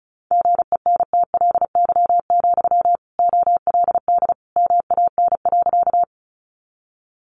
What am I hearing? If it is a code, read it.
Morse code, heard as a keyed tone, about 34 wpm, 700 Hz. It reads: GENTLY, OLD MAN.
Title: Sentences from Top 1500 Words 35wpm